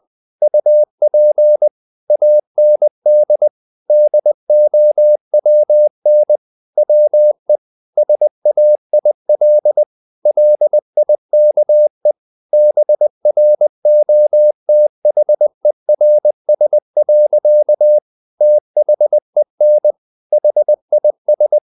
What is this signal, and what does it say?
Morse code, heard as a keyed tone, about 20 wpm, 600 Hz. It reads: UP AND DOWN WE SAIL LIKE BROTHERS. THEN HIS
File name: the old man and the sea part 2 20wpm